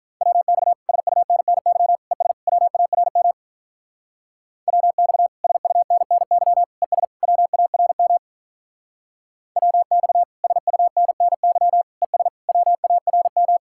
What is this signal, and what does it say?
Morse code, heard as a keyed tone, about 44 wpm, 700 Hz. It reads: WX SUNNY ES WARM WX SUNNY ES WARM WX SUNNY ES WARM